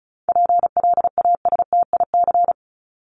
Morse code, heard as a keyed tone, about 34 wpm, 700 Hz. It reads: PLASTIC
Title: Top 1001-1250 Words 35wpm